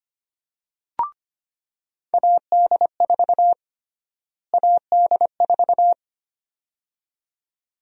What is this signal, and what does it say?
Morse code, heard as a keyed tone, about 25 wpm, 700 Hz. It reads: AD4 AD4
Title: USA Call Sign Prefixes - Repeated 2x - 25wpm